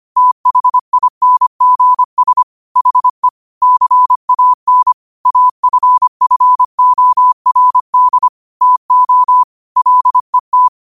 Morse code, heard as a keyed tone, about 25 wpm, 1000 Hz. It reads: THINGS HE CAN AFFORD TO LET
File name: qotdfiles / 25wpm